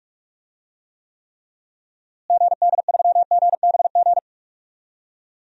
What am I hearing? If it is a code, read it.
Morse code, heard as a keyed tone, about 46 wpm, 700 Hz. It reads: GD3GBG
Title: DXCC-100 Call Signs - 6 Characters - Repeated 3x - 45wpm